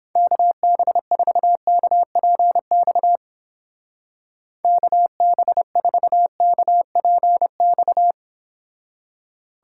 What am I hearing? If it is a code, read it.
Morse code, heard as a keyed tone, about 29 wpm, 700 Hz. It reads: KB4KPX KB4KPX